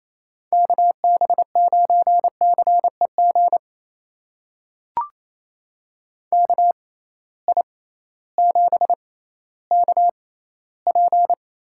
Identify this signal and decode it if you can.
Morse code, heard as a keyed tone, about 28 wpm, spaced as for 15 wpm, 700 Hz. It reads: KB9CEG KI7KP
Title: USA Call Signs - 6 Characters - 6x Character Spacing - 28wpm